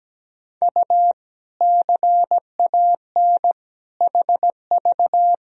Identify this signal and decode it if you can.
Morse code, heard as a keyed tone, about 17 wpm, 700 Hz. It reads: U CAN HV
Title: Sentences from Top 100 Words and Abbreviations 17wpm